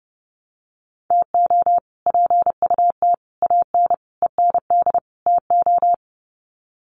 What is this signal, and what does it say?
Morse code, heard as a keyed tone, about 30 wpm, 700 Hz. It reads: TO PUT AN END TO
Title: Sets of 5 Words Encore2 30wpm